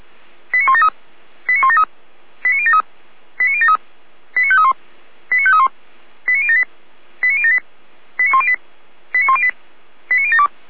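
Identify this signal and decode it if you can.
Five-tone selective call (ZVEI), heard as quick runs of five tones, tones of about 70 ms. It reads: 89173 89173 89073 89073 89531 89531 89078 89078 89108 89108 89072
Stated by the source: ZVEI five-tone selective call received on SDR.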